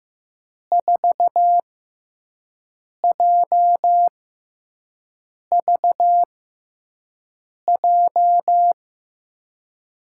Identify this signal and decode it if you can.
Morse code, heard as a keyed tone, about 15 wpm, spaced as for 5 wpm, 700 Hz. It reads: 4JVJ
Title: USA Call Signs - 6x Character Spacing - 15wpm